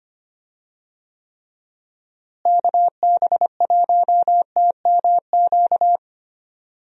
Morse code, heard as a keyed tone, about 25 wpm, 700 Hz. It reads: KB1TMQ
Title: USA Call Signs - 6 Characters - Repeated 2x - 25wpm